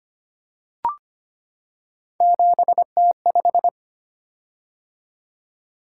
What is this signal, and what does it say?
Morse code, heard as a keyed tone, about 25 wpm, 700 Hz. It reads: 7T5
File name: Three Letter-Number Combinations - 25wpm